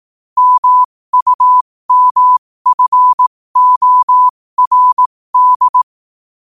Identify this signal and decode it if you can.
Morse code, heard as a keyed tone, about 18 wpm, spaced as for 15 wpm, 1000 Hz. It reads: MUMFORD
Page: New quotes every day in morse code at 15 Words per minute.